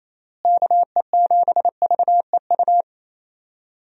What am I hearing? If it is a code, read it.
Morse code, heard as a keyed tone, about 28 wpm, 700 Hz. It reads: KE7VEU
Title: USA Call Signs - 6 Characters - Repeated 2x - 28wpm